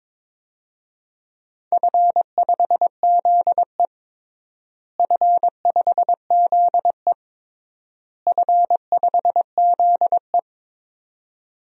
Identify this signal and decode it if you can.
Morse code, heard as a keyed tone, about 22 wpm, 700 Hz. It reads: F5ZE F5ZE F5ZE
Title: DXCC-100 Call Signs - 4 Characters - Repeated 3x - 22wpm